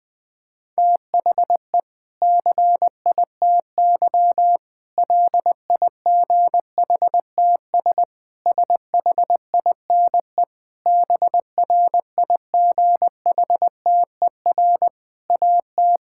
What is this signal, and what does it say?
Morse code, heard as a keyed tone, about 20 wpm, 700 Hz. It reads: THE CITY LIGHTS SHINE BRIGHTER AT
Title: Difficult Sentences with Hints - 20wpm